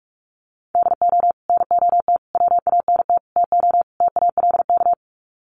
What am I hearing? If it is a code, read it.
Morse code, heard as a keyed tone, about 44 wpm, 700 Hz. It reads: DO NOT WANT TO TALK
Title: Sets of 5 Words 45wpm